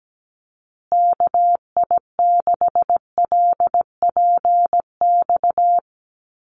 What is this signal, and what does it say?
Morse code, heard as a keyed tone, about 17 wpm, 700 Hz. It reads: KI6LPX
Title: USA Call Signs Repeated 3x 17wpm